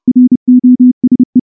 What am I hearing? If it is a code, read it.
Morse code, heard as a keyed tone, about 28 wpm, 250 Hz. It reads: ROSE
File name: morse / morse-training-30WPM